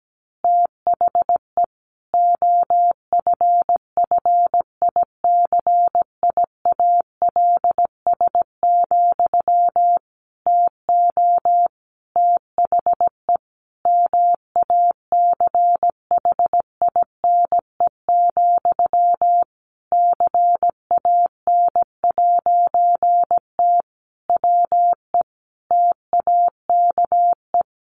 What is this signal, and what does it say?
Morse code, heard as a keyed tone, about 17 wpm, 700 Hz. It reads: THE OFFICIALS, TO THE MACHINE, CAN'T WE TAKE
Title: A Sound Of Thunder 17wpm